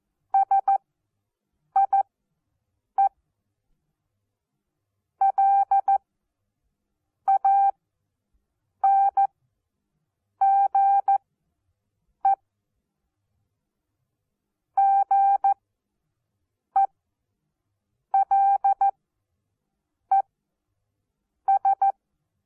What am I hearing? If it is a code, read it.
Morse code, heard as a keyed tone, about 14 wpm, spaced as for 6 wpm, 790 Hz. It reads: SIE LANGE GELES
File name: Onlinelehrgang / morsen / DJ4UF Morsen14